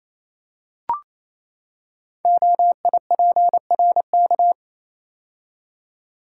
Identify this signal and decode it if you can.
Morse code, heard as a keyed tone, about 28 wpm, 700 Hz. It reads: OIPRK
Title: Code Groups - 5 Letters - 28wpm